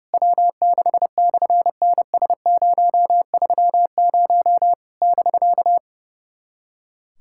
Morse code, heard as a keyed tone, about 30 wpm, 700 Hz. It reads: W6/NS030 <BK>
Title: SOTA QSO Practice 30wpm